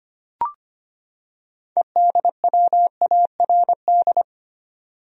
Morse code, heard as a keyed tone, about 25 wpm, 700 Hz. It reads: EDWARD
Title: USA Common Names - 25wpm